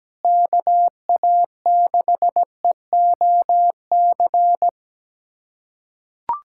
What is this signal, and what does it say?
Morse code, heard as a keyed tone, about 17 wpm, 700 Hz. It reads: KA6EOC
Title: USA Callsign DE Callsign 17wpm